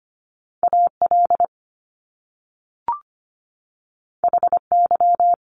Morse code, heard as a keyed tone, about 25 wpm, 700 Hz. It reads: AL HY
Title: Top Two Letter Combinations in Words 25wpm